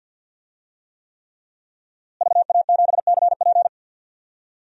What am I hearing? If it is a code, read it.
Morse code, heard as a keyed tone, about 46 wpm, 700 Hz. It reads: VA7CP
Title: DXCC-100 Call Signs - 5 Characters - Repeated 3x - 50wpm